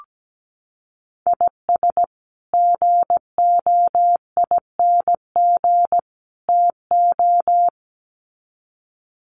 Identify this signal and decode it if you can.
Morse code, heard as a keyed tone, about 17 wpm, 700 Hz. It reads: IS GOING TO
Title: Sets of 3 Words 17wpm